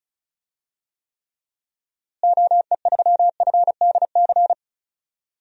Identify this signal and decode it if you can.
Morse code, heard as a keyed tone, about 34 wpm, 700 Hz. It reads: OE3FDC
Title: DXCC-100 Call Signs - 6 Characters - Repeated 2x - 35wpm